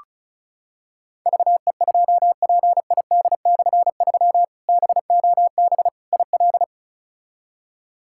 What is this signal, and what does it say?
Morse code, heard as a keyed tone, about 36 wpm, 700 Hz. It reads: VE2PID/3 BOB IL
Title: K1USN SST QSO Practice - 35wpm